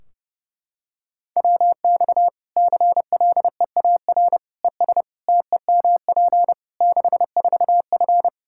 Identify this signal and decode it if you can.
Morse code, heard as a keyed tone, about 32 wpm, 700 Hz. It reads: WX CLEAR ES TEMP 64F
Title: K1USN SST QSO Practice 30wpm